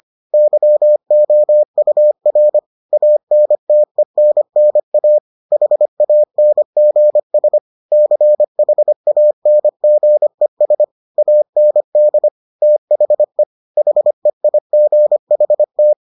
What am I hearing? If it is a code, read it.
Morse code, heard as a keyed tone, about 25 wpm, 600 Hz. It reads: YOUR ANTENNA HANGS CHANGES AND THE HEIGHT